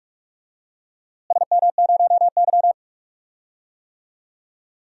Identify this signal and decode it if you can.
Morse code, heard as a keyed tone, about 46 wpm, 700 Hz. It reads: SM0Y
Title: DXCC-100 Call Signs - 4 Characters - Repeated 2x - 45wpm